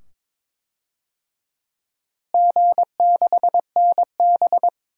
Morse code, heard as a keyed tone, about 21 wpm, 700 Hz. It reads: G6NB
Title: DXCC-100 Call Signs - 4 Characters - Repeated 2x - 22wpm